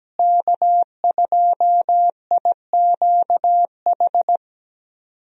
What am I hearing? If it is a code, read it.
Morse code, heard as a keyed tone, about 17 wpm, 700 Hz. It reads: K2IQH
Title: USA Call Signs - 5 Characters - Repeated 3x - 17wpm